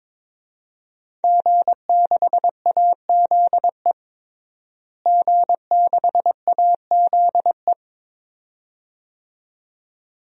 Morse code, heard as a keyed tone, about 22 wpm, 700 Hz. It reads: G6AZE G6AZE
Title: DXCC-100 Call Signs - 5 Characters - Repeated 2x - 22wpm